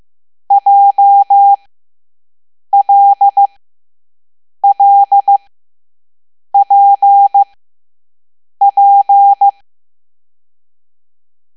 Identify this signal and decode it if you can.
Morse code, heard as a keyed tone, about 15 wpm, spaced as for 6 wpm, 800 Hz. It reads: JLLPP
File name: Morse / Level7B